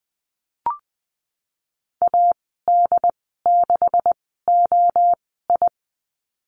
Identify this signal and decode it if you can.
Morse code, heard as a keyed tone, about 20 wpm, spaced as for 15 wpm, 700 Hz. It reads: AD6OI
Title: 140 Lesson 23g - Letter f - Call Signs 20wpm